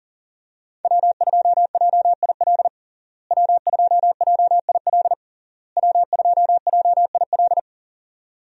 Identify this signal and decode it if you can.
Morse code, heard as a keyed tone, about 40 wpm, 700 Hz. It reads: W2JIL W2JIL W2JIL
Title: USA Call Signs - 5 Characters - Repeated 3x - 40wpm